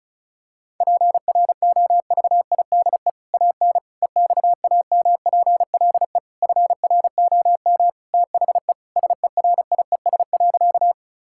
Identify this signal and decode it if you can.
Morse code, heard as a keyed tone, about 36 wpm, 700 Hz. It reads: PROVIDE AN EXAMPLE FROM THE SERIES.
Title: Sentences from Top 1000 Words - Review Part 5 - 35wpm